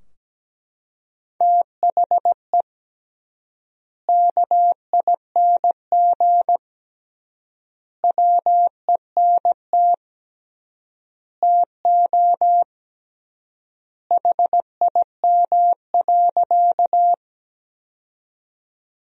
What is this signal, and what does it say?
Morse code, heard as a keyed tone, about 17 wpm, 700 Hz. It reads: THE KING WENT TO HIM.
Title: 3x Word-Spacing - Sentences from Top 400 Words - 17wpm